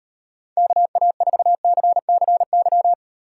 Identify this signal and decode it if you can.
Morse code, heard as a keyed tone, about 38 wpm, 700 Hz. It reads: KA4CCY